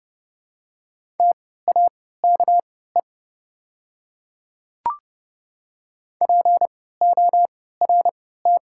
Morse code, heard as a keyed tone, about 30 wpm, spaced as for 17 wpm, 690 Hz. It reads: TAKE PORT
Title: 3x Character Spacing - Top 200 Words - 30wpm